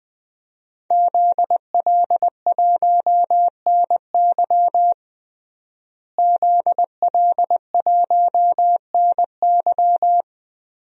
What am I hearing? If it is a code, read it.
Morse code, heard as a keyed tone, about 20 wpm, 700 Hz. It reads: ZL1NY ZL1NY
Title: DXCC-100 Call Signs - 5 Characters - Repeated 3x - 20wpm